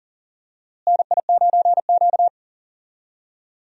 Morse code, heard as a keyed tone, about 40 wpm, 700 Hz. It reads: NI9Q
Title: USA Call Signs - 4 Characters - Repeated 2x - 40wpm